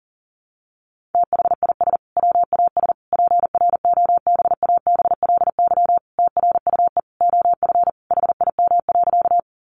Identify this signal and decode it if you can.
Morse code, heard as a keyed tone, about 40 wpm, 700 Hz. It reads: THIS WAS PROBABLY TRUE OF HIM.
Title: Sentences from Top 700 Words 40wpm